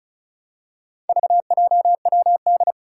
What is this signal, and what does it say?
Morse code, heard as a keyed tone, about 36 wpm, 700 Hz. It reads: VJWD
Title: Four Challenging Letter-Number Combinations 35wpm